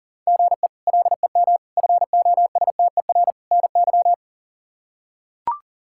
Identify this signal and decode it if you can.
Morse code, heard as a keyed tone, about 37 wpm, 700 Hz. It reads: GE LEM FOSTER NY
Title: K1USN SST QSO Practice - 40wpm